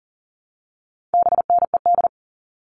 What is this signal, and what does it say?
Morse code, heard as a keyed tone, about 38 wpm, 700 Hz. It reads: BNED
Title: All Cut Numbers - 4-Digit 40wpm